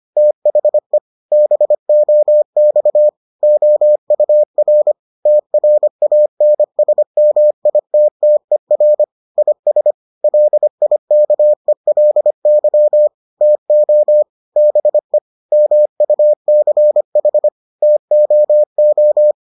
Morse code, heard as a keyed tone, about 25 wpm, 600 Hz. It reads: THE BOX OUR TRANSMITTER IS LIKELY TO BE MUCH TOO